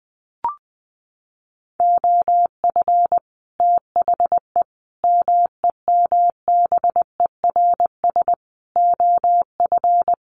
Morse code, heard as a keyed tone, about 20 wpm, 700 Hz. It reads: OF THE MEMBERS OF
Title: Sets of 5 Words Encore2 20wpm